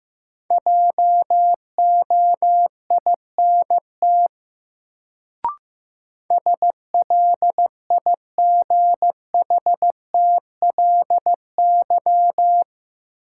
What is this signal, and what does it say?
Morse code, heard as a keyed tone, about 15 wpm, 700 Hz. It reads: JOINT SLIGHTLY
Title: Sentences from Top 1300 Words - New Word Drill - 15wpm